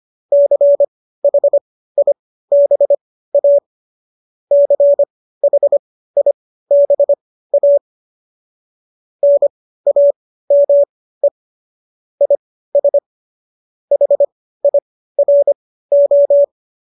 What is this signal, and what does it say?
Morse code, heard as a keyed tone, about 25 wpm, spaced as for 15 wpm, 570 Hz. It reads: CHIBA CHIBA NAME IS HIRO
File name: qso208 15